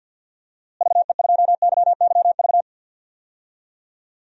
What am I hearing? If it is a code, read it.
Morse code, heard as a keyed tone, about 53 wpm, 700 Hz. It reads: VE2YYV
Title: DXCC-100 Call Signs - 6 Characters - Repeated 2x - 50wpm